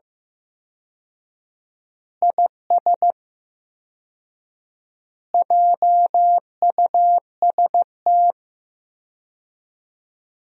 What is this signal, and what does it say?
Morse code, heard as a keyed tone, about 15 wpm, 700 Hz. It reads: IS JUST
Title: 4x Word-Spacing - Sentences from Top 200 Words - 15wpm